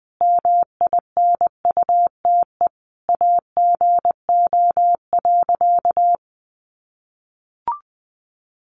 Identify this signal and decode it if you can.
Morse code, heard as a keyed tone, about 19 wpm, 700 Hz. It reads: MINUTE AGO.
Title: Sentences from Top 500 Words 20wpm